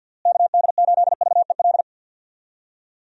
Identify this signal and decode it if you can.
Morse code, heard as a keyed tone, about 52 wpm, 700 Hz. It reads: KD8VEL
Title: USA Call Signs - 6 Characters - Repeated 3x - 50wpm